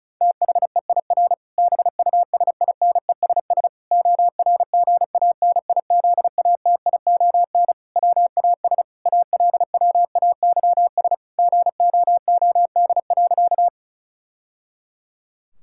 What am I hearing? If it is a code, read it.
Morse code, heard as a keyed tone, about 36 wpm, 700 Hz. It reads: THEIR BUSINESS ORGANIZATION WAS ALWAYS GOOD.
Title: Sentences from Top 1000 Words - Review Part 5 - 35wpm